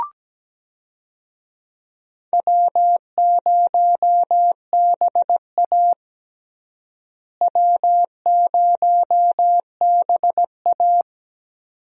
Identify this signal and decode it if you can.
Morse code, heard as a keyed tone, about 17 wpm, 700 Hz. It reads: W0BA W0BA
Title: USA Call Signs - 4 Characters - Repeated 2x - 17wpm